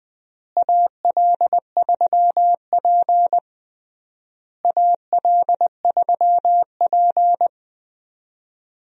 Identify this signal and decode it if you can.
Morse code, heard as a keyed tone, about 20 wpm, 700 Hz. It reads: AL3P AL3P
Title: USA Call Signs - 4 Characters - Repeated 3x - 20wpm